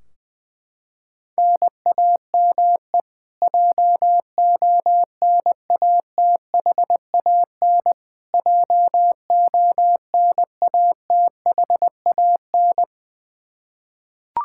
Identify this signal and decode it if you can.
Morse code, heard as a keyed tone, about 21 wpm, 700 Hz. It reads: NAME JONATHAN JONATHAN
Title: Practice QSO - Exchange 1 - Beginning - 20wpm